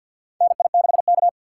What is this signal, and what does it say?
Morse code, heard as a keyed tone, about 53 wpm, 700 Hz. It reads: NI6K
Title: USA Call Signs - 2x Character Spacing - 50wpm